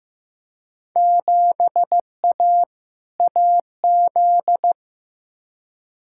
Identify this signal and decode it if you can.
Morse code, heard as a keyed tone, about 15 wpm, 700 Hz. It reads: 7A AZ
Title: Field Day Exchanges - 15wpm